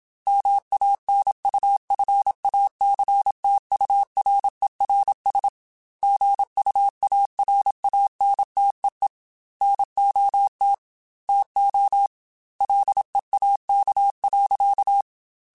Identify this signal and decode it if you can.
Morse code, heard as a keyed tone, about 26 wpm, 800 Hz. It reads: MANUFACTURERS GUARANTEE NOT TO LEAK.
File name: Quotes / Quotes-mp3s / Airtight @25